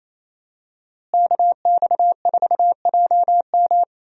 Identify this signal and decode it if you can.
Morse code, heard as a keyed tone, about 28 wpm, 700 Hz. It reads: KX4JM
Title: USA Call Signs - 5 Characters - Repeated 3x - 28wpm